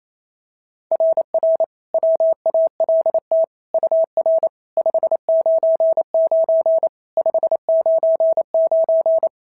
Morse code, heard as a keyed tone, about 25 wpm, 650 Hz. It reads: RR WALT UR 599 599
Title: SOTA QSO Practice 28wpm